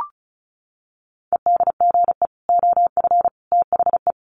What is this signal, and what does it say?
Morse code, heard as a keyed tone, about 34 wpm, 700 Hz. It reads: EDGE OF THE
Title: Sets of 3 Words 35wpm